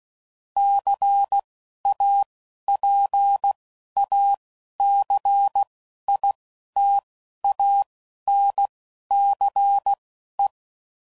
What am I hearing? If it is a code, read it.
Morse code, heard as a keyed tone, about 16 wpm, spaced as for 11 wpm, 800 Hz. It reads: CAPACITANCE
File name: CW Code Groups @10WPM / Technical Subjects / Reactance